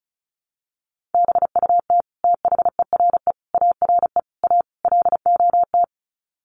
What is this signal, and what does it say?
Morse code, heard as a keyed tone, about 34 wpm, 700 Hz. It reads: BUT THERE ARE A LOT
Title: Sets of 5 Words Encore2 35wpm